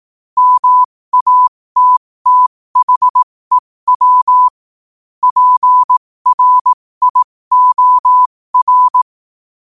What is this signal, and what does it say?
Morse code, heard as a keyed tone, about 18 wpm, spaced as for 15 wpm, 1000 Hz. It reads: MATTHEW PRIOR
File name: qotdfiles / 15wpm